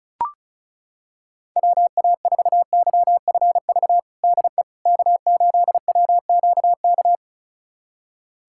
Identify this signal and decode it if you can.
Morse code, heard as a keyed tone, about 36 wpm, 700 Hz. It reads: WA4YFV DE K8WQK
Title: USA Callsign DE Callsign 35wpm